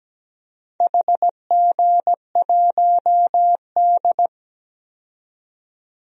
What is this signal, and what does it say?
Morse code, heard as a keyed tone, about 17 wpm, 700 Hz. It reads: HG1D
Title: DXCC-100 Call Signs - 4 Characters - 17wpm